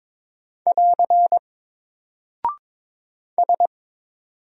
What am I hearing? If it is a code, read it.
Morse code, heard as a keyed tone, about 22 wpm, 700 Hz. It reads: <AR> S
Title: Single Character (Letter:Number:Punctuation:Prosign) 22wpm